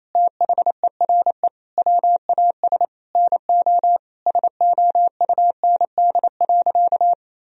Morse code, heard as a keyed tone, about 28 wpm, 700 Hz. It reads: THERE WAS NO SOUND.
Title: Sentences from Top 1000 Words - Review Part 2 - 28wpm